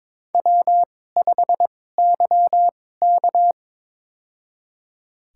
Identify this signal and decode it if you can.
Morse code, heard as a keyed tone, about 21 wpm, spaced as for 16 wpm, 700 Hz. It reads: W5YK
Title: USA Call Signs - 4 Characters - 2x Character Spacing - 22wpm